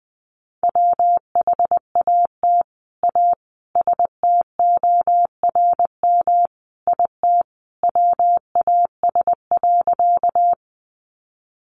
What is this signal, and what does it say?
Morse code, heard as a keyed tone, about 20 wpm, 700 Hz. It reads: WHAT A STORM IT WAS.
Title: Sentences from Top 1500 Words 20wpm